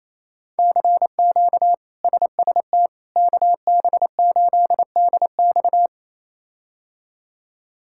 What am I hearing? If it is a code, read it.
Morse code, heard as a keyed tone, about 28 wpm, 700 Hz. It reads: CQ SST KB8DX
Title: K1USN SST QSO Practice - 28wpm